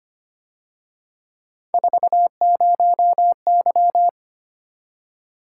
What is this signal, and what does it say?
Morse code, heard as a keyed tone, about 29 wpm, 700 Hz. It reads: 40Y
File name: Speed-Racing - Three Letter-Number Combinations - 25wpm